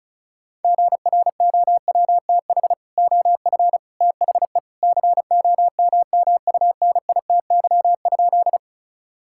Morse code, heard as a keyed tone, about 36 wpm, 700 Hz. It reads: GROWTH OF THE COMMUNITY?
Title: Sentences from Top 1000 Words - Review Part 5 - 35wpm